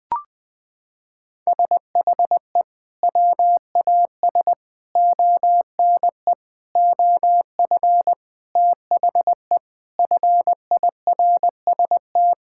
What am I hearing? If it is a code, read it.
Morse code, heard as a keyed tone, about 21 wpm, 700 Hz. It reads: SHE WAS ONE OF THE FIRST
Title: Sentences with 5-Word Phrases - 20wpm